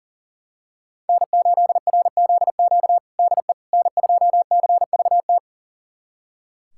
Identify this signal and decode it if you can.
Morse code, heard as a keyed tone, about 40 wpm, 700 Hz. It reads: N8RZQ DE N2CVT
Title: USA Callsign DE Callsign - 5 Characters - 40wpm